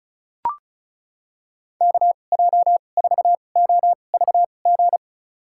Morse code, heard as a keyed tone, about 34 wpm, spaced as for 25 wpm, 700 Hz. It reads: KJ4OVG
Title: USA Call Signs - 2x Character Spacing - 35wpm